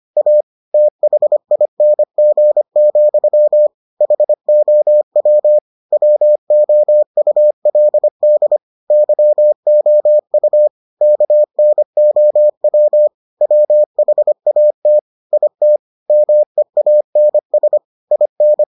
Morse code, heard as a keyed tone, about 25 wpm, 600 Hz. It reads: A THING, HOW WOULD YOU KNOW WHAT IT MEANS IN